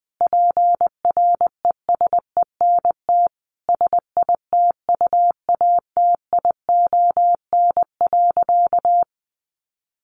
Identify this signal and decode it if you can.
Morse code, heard as a keyed tone, about 20 wpm, 700 Hz. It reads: PRESENT SITUATION.
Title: Sentences from Top 1200 Words 20wpm